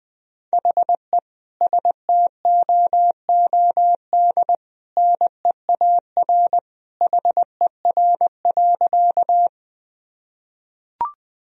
Morse code, heard as a keyed tone, about 20 wpm, 700 Hz. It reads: HE STOOD NEAR HER.
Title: Sentences from Top 1000 Words - Review Part 2 - 20wpm